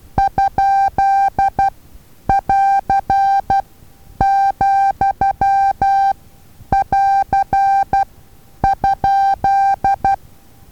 Morse code, heard as a keyed tone, about 12 wpm, spaced as for 9 wpm, 790 Hz. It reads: ?<AR>,<AR>?
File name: Taviro HA6VA A